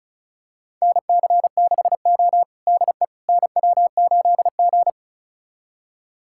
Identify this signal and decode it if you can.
Morse code, heard as a keyed tone, about 36 wpm, 700 Hz. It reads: NC6O DE NW8G
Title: USA Callsign DE Callsign - 4 Characters - 35wpm